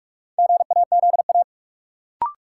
Morse code, heard as a keyed tone, about 46 wpm, 700 Hz. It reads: GAZA
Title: Sets of 5 Words - New Word Drill - 45wpm